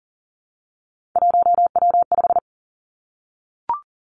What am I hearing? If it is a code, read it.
Morse code, heard as a keyed tone, about 40 wpm, 700 Hz. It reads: 1W5
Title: Three Challenging Letter-Number Combinations 40wpm